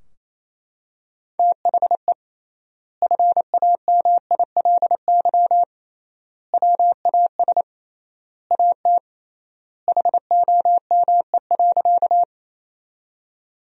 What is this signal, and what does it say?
Morse code, heard as a keyed tone, about 28 wpm, 700 Hz. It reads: THE FAMILY WAS AT HOME.
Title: 3x Word-Spacing - Sentences from Top 400 Words - 28wpm